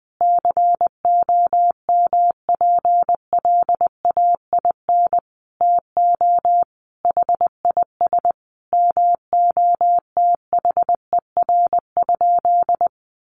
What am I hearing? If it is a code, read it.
Morse code, heard as a keyed tone, about 20 wpm, 700 Hz. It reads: COMPLAIN TO HIS MOTHER?
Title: Sentences from Top 1300 Words 20wpm